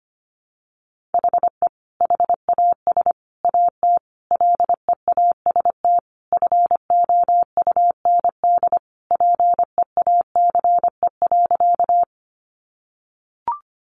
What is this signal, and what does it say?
Morse code, heard as a keyed tone, about 25 wpm, 700 Hz. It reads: HE HAS AT LEAST FOUND PEACE.
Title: Sentences from Top 1500 Words 25wpm